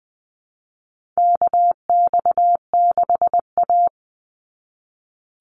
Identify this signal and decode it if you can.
Morse code, heard as a keyed tone, about 20 wpm, 700 Hz. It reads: KX6A
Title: NA SOTA Chaser Callsigns 20wpm